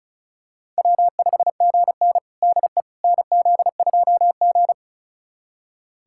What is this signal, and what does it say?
Morse code, heard as a keyed tone, about 34 wpm, 700 Hz. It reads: W5GN DE NZ2G
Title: USA Callsign DE Callsign - 4 Characters - 35wpm